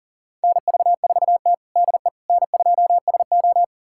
Morse code, heard as a keyed tone, about 40 wpm, 690 Hz. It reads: NV4T DE N2SO
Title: USA Callsign DE Callsign - 4 Characters - 40wpm